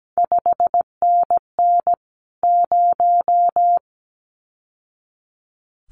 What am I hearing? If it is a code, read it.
Morse code, heard as a keyed tone, about 17 wpm, 700 Hz. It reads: 5NN 0
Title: CQ WW DX Contest Exchange Encore - 17wpm